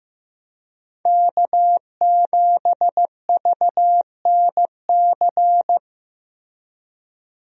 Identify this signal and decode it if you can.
Morse code, heard as a keyed tone, about 15 wpm, 700 Hz. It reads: K7VNC
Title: USA Call Signs - 5 Characters - Repeated 3x - 15wpm